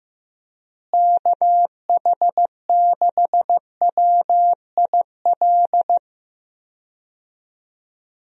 Morse code, heard as a keyed tone, about 15 wpm, 700 Hz. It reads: KH6WIL
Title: USA Call Signs - 6 Characters - Repeated 3x - 15wpm